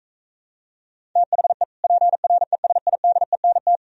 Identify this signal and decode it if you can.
Morse code, heard as a keyed tone, about 42 wpm, 700 Hz. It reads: THE PRESIDENT